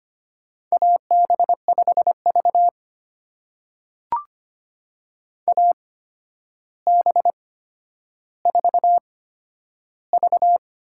Morse code, heard as a keyed tone, about 25 wpm, spaced as for 11 wpm, 700 Hz. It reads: AB5V AB4V
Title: USA Call Signs - 4 Characters - 8x Character Spacing - 25wpm